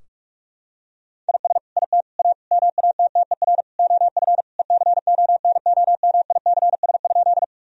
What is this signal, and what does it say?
Morse code, heard as a keyed tone, about 46 wpm, 700 Hz. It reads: IS IT A MATTER OF ECONOMICS?
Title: Sentences from Top 1000 Words - Review Part 3 - 45wpm